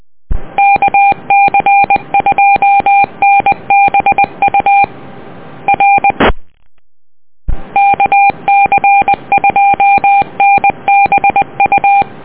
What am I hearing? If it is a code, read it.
Morse code, heard as a keyed tone, about 20 wpm, 800 Hz. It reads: KC2NBU R KC2NBU